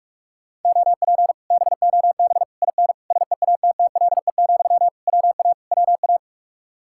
Morse code, heard as a keyed tone, about 45 wpm, 700 Hz. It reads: OP BOB IN SEATTLE, WA WA